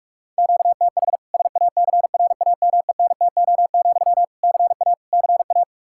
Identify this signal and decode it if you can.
Morse code, heard as a keyed tone, about 46 wpm, 700 Hz. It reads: QTH SACRAMENTO, CA CA
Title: Practice QSO - Exchange 1 - Beginning - 45wpm